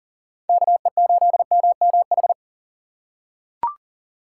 Morse code, heard as a keyed tone, about 40 wpm, 700 Hz. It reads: KE8MMH
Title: USA Call Signs - 6 Characters - 2x Character Spacing - 40wpm